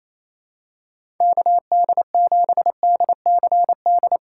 Morse code, heard as a keyed tone, about 28 wpm, 700 Hz. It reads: KD7DCD
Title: USA Call Signs - 4x Character Spacing - 28wpm